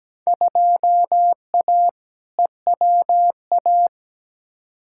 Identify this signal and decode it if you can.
Morse code, heard as a keyed tone, about 17 wpm, 700 Hz. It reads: 2A EWA
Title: Field Day Exchanges - 17wpm